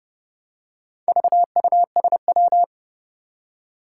Morse code, heard as a keyed tone, about 30 wpm, 700 Hz. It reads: VUSW
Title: Four Challenging Letter-Number Combinations 30wpm